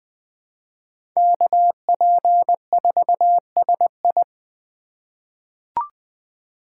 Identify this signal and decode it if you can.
Morse code, heard as a keyed tone, about 20 wpm, 700 Hz. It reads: KP4SI
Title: USA Call Signs - 5 Characters - 3x Character Spacing - 20wpm